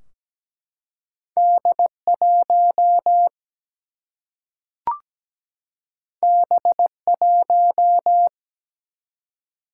Morse code, heard as a keyed tone, about 17 wpm, 700 Hz. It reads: D1 B1
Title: Single-Letter Single-Number 17wpm